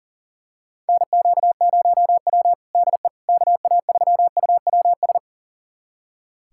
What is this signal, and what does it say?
Morse code, heard as a keyed tone, about 40 wpm, 700 Hz. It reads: NQ0W DE KA3UWS
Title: USA Callsign DE Callsign 40wpm